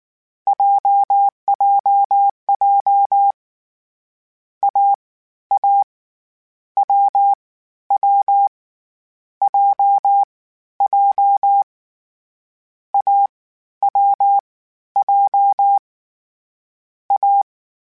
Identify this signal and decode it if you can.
Morse code, heard as a keyed tone, about 19 wpm, 800 Hz. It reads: JJJ A A W W J J A W J A
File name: Difficult Combos / Difficult Combos-mp3s / AWJ @18